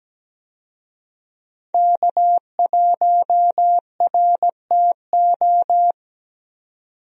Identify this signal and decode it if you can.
Morse code, heard as a keyed tone, about 17 wpm, 700 Hz. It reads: K1RTO
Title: USA Call Signs - 5 Characters - 17wpm